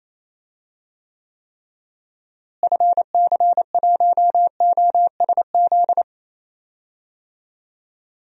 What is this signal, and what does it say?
Morse code, heard as a keyed tone, about 28 wpm, 700 Hz. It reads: FC1OSZ
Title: DXCC-100 Call Signs - 6 Characters - Repeated 3x - 28wpm